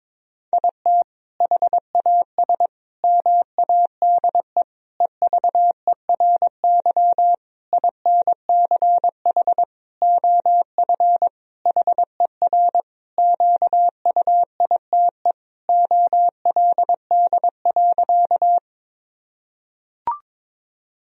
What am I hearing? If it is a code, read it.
Morse code, heard as a keyed tone, about 22 wpm, 700 Hz. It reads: IT HAS MADE EVERY INCH OF HER QUITE OLD.
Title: Sentences from Top 1000 Words - Review Part 3 - 22wpm